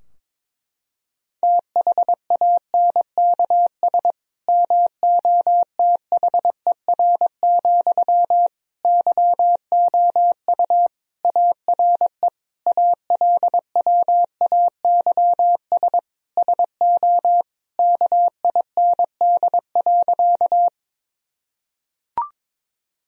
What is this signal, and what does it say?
Morse code, heard as a keyed tone, about 22 wpm, 700 Hz. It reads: THANKS MOTHER, YOU ARE ALWAYS SO KIND.
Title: Sentences from Top 1000 Words - Review Part 3 - 22wpm